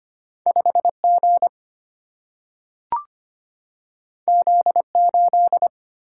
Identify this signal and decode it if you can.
Morse code, heard as a keyed tone, about 25 wpm, 700 Hz. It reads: 5 G Z 8
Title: Single-Letter Single-Number 25wpm